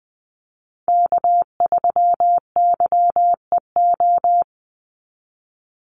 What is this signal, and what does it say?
Morse code, heard as a keyed tone, about 20 wpm, 700 Hz. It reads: K3YEO
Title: Call Signs Repeated 2x - 20wpm